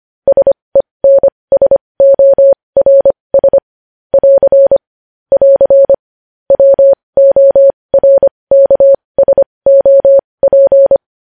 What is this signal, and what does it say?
Morse code, heard as a keyed tone, about 25 wpm, spaced as for 20 wpm, 550 Hz. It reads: SENSORS <AR> <AR> WORKSOP
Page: News Headlines in Morse Code at 20 WPM